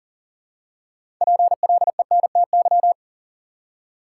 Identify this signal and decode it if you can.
Morse code, heard as a keyed tone, about 40 wpm, 700 Hz. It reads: PLENTY
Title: Six Letter Words - 40wpm